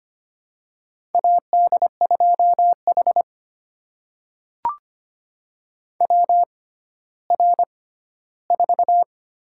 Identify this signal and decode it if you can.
Morse code, heard as a keyed tone, about 25 wpm, spaced as for 14 wpm, 700 Hz. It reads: AD2H WR4
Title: USA Call Signs - 4 Characters - 6x Character Spacing - 25wpm